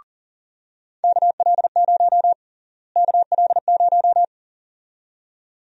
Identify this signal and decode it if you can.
Morse code, heard as a keyed tone, about 40 wpm, 700 Hz. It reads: KL0 KL0
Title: USA Call Sign Prefixes - Repeated 2x - 40wpm